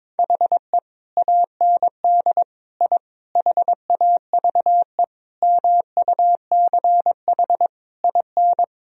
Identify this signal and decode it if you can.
Morse code, heard as a keyed tone, about 22 wpm, 690 Hz. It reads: HE AND I HAVE MUCH IN
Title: Sentences from Top 1000 Words - Review Part 3 - 22wpm